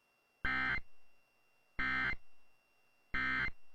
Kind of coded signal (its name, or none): Emergency Alert System (SAME) tones